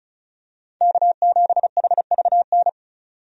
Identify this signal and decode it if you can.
Morse code, heard as a keyed tone, about 34 wpm, 700 Hz. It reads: K7HVN
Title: USA Call Signs - 5 Characters - 35wpm